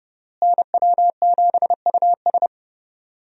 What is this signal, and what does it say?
Morse code, heard as a keyed tone, about 32 wpm, 700 Hz. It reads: NW7US
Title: USA Call Signs - 5 Characters - Repeated 3x - 30wpm